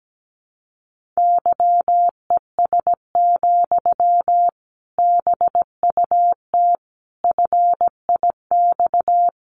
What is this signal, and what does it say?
Morse code, heard as a keyed tone, about 17 wpm, 700 Hz. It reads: YES, BUT FIX
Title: Sentences from Top 1400 Words 17wpm